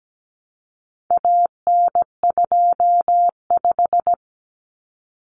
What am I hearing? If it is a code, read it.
Morse code, heard as a keyed tone, about 17 wpm, 700 Hz. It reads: AN25
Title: Common Cut Numbers - 4-Digit 17wpm